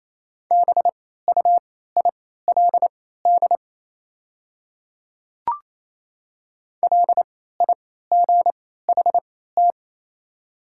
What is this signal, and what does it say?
Morse code, heard as a keyed tone, about 28 wpm, spaced as for 16 wpm, 700 Hz. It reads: BUILD LIGHT
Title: 3x Character Spacing - Top 200 Words - 28wpm